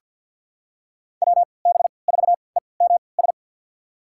Morse code, heard as a keyed tone, about 46 wpm, spaced as for 27 wpm, 690 Hz. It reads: WB4EMS
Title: USA Call Signs - 6 Characters - 3x Character Spacing - 50wpm